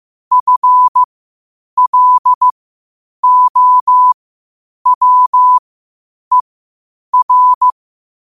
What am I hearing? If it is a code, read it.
Morse code, heard as a keyed tone, about 15 wpm, spaced as for 8 wpm, 1000 Hz. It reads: FLOWER
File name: qotdfiles / 8wpm